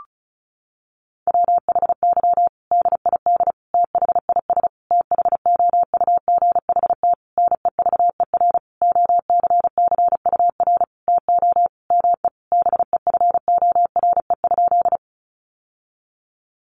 Morse code, heard as a keyed tone, about 34 wpm, 700 Hz. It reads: WHY DID THIS THOUGHT NEVER OCCUR TO ME BEFORE?
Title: Sentences from Top 900 Words 35wpm